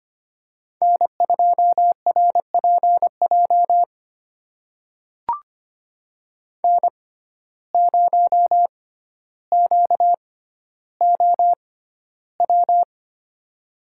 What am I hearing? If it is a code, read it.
Morse code, heard as a keyed tone, about 25 wpm, spaced as for 13 wpm, 700 Hz. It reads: N2RPJ N0QOW
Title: USA Call Signs - 5 Characters - 6x Character Spacing - 25wpm